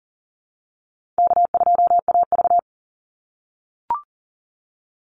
Morse code, heard as a keyed tone, about 40 wpm, 700 Hz. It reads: K2AV
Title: Call Signs - 40wpm